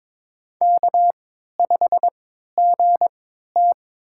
Morse code, heard as a keyed tone, about 22 wpm, spaced as for 12 wpm, 700 Hz. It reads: K5GT
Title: USA Call Signs - 4 Characters - 3x Character Spacing - 22wpm